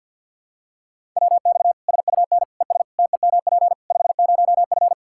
Morse code, heard as a keyed tone, about 52 wpm, 700 Hz. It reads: WX SUN ES TEMP 50F
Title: Practice QSO - Exchange 2 - Info - 50wpm